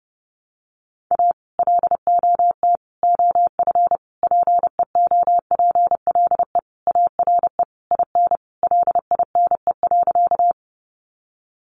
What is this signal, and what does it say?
Morse code, heard as a keyed tone, about 30 wpm, 700 Hz. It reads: A LOT OF PEOPLE ARE IN LINE.
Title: Additional Easy Phrases 30wpm